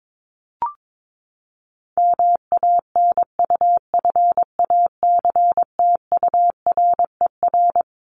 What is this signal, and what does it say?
Morse code, heard as a keyed tone, about 22 wpm, 700 Hz. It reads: MANUFACTURER
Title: Top 1751-2000 Words 22wpm